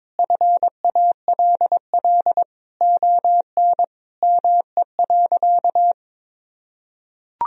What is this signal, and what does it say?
Morse code, heard as a keyed tone, about 22 wpm, 700 Hz. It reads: FALL ON ME.
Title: Sentences from Top 1000 Words - Review Part 2 - 22wpm